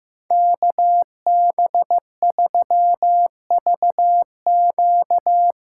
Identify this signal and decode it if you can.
Morse code, heard as a keyed tone, about 15 wpm, 700 Hz. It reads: KB3VQ
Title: USA Call Signs - 6 Characters - 8x Character Spacing - 15wpm